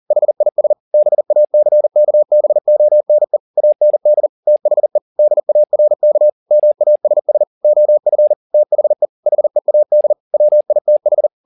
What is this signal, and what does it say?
Morse code, heard as a keyed tone, about 40 wpm, 600 Hz. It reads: HIS BACKBONE AND THE DARK MASS OF THE HEAD WITH
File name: the old man and the sea 2 40wpm